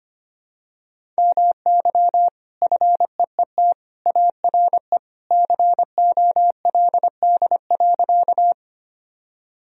Morse code, heard as a keyed tone, about 25 wpm, 700 Hz. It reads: MY FEET ARE COLD.
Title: Sentences from Top 1000 Words - Review Part 1 - 25wpm